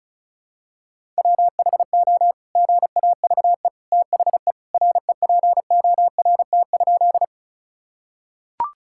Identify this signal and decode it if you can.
Morse code, heard as a keyed tone, about 36 wpm, 700 Hz. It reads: WHO GAVE THE REPORT?
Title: Sentences from Top 1000 Words - Review Part 6 - 35wpm